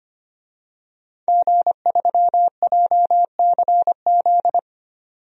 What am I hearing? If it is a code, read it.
Morse code, heard as a keyed tone, about 25 wpm, 700 Hz. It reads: G3JCZ
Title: DXCC-100 Call Signs - 5 Characters - 25wpm